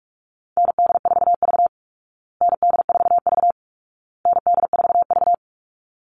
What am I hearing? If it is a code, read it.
Morse code, heard as a keyed tone, about 44 wpm, 700 Hz. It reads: ND4V ND4V ND4V
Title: Call Signs Repeated 2x - 45wpm